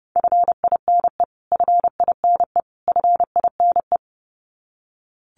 Morse code, heard as a keyed tone, about 30 wpm, 700 Hz. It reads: FINE FINE FINE
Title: Top 500 Words Repeated 30wpm